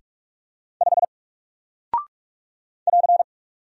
Morse code, heard as a keyed tone, about 45 wpm, 700 Hz. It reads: 5 <AR>
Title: Single Character (Letter:Number:Punctuation:Prosign) Alt 45wpm